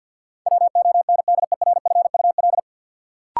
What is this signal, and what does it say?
Morse code, heard as a keyed tone, about 50 wpm, 700 Hz. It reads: WONDERFUL
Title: Sentences from Top 1100 Words - New Word Drill - 50wpm